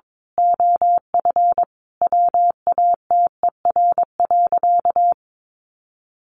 Morse code, heard as a keyed tone, about 22 wpm, 700 Hz. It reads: OF WATER.
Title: Sentences from Top 1300 Words 22wpm